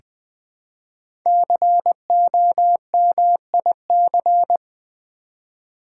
Morse code, heard as a keyed tone, about 20 wpm, 700 Hz. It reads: COMIC
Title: Build A Word - 1 Letter At A Time - 3 to 5-Letter Words - 20wpm